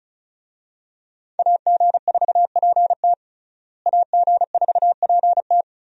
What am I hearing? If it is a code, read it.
Morse code, heard as a keyed tone, about 34 wpm, 700 Hz. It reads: AG4PT AG4PT
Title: USA Call Signs - 5 Characters - Repeated 2x - 35wpm